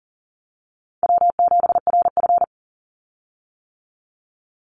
Morse code, heard as a keyed tone, about 40 wpm, 700 Hz. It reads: W7RF
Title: Call Signs - 40wpm